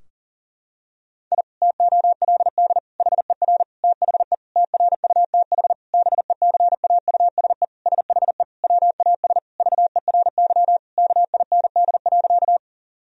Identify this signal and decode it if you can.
Morse code, heard as a keyed tone, about 40 wpm, 700 Hz. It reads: I TOLD HER THE TRUTH BECAUSE SHE WAS VERY KIND.
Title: Sentences from Top 1000 Words - Review Part 3 - 40wpm